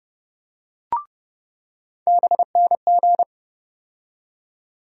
Morse code, heard as a keyed tone, about 30 wpm, 700 Hz. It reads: BNG
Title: USA Call Sign Postfixes - 3 Letters 30wpm